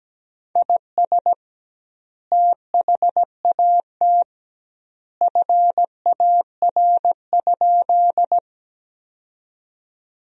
Morse code, heard as a keyed tone, about 17 wpm, 700 Hz. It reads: IS THAT FAR?
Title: 2x Word-Spacing - Sentences from Top 300 Words - 17wpm